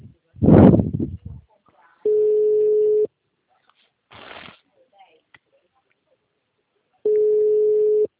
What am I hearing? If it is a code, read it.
Ringback tone (Europe, 425 Hz), with speech.